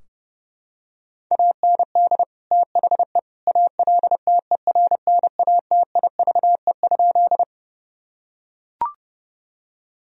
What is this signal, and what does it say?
Morse code, heard as a keyed tone, about 30 wpm, 700 Hz. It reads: AND THE ALTERNATIVE?
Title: Sentences from Top 1000 Words - Review Part 4 - 30wpm